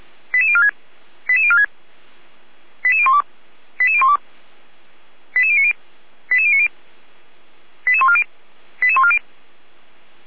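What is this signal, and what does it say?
Five-tone selective call (ZVEI), heard as quick runs of five tones, tones of about 70 ms. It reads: 88046 88046 88012 88012 88090 88090 88140 88140
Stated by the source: ZVEI five-tone selective call 88149, received by SDR, fire brigade feuerwehr-orlamuende.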